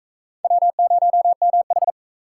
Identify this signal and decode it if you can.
Morse code, heard as a keyed tone, about 36 wpm, 700 Hz. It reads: W0MH